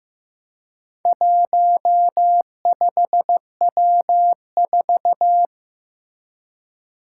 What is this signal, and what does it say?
Morse code, heard as a keyed tone, about 15 wpm, 700 Hz. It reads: 15W4
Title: Four Challenging Letter-Number Combinations 15wpm